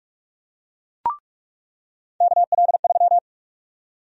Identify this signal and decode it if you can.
Morse code, heard as a keyed tone, about 46 wpm, 700 Hz. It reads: KL3
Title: USA Call Sign Prefixes - 45wpm